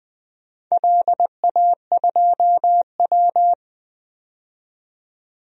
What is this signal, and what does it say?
Morse code, heard as a keyed tone, about 20 wpm, 700 Hz. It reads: LA2W
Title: DXCC-100 Call Signs - 4 Characters - 20wpm